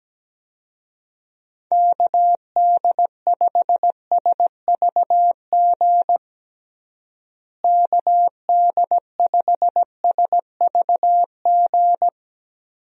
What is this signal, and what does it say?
Morse code, heard as a keyed tone, about 17 wpm, 700 Hz. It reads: KD5SVG KD5SVG
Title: USA Call Signs - 6 Characters - Repeated 2x - 17wpm